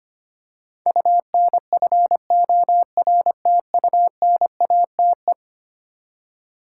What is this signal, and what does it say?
Morse code, heard as a keyed tone, about 25 wpm, 700 Hz. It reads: UNFORTUNATE
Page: morse-code-ninja-practice-25wpm